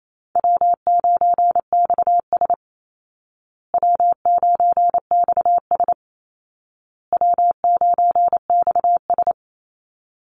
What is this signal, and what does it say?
Morse code, heard as a keyed tone, about 28 wpm, 700 Hz. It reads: W9XS W9XS W9XS
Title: Call Signs Repeated 2x - 28wpm